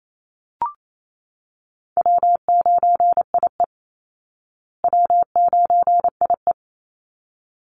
Morse code, heard as a keyed tone, about 28 wpm, 700 Hz. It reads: W9IE W9IE
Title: Easy Call Signs Repeated 2x - 28wpm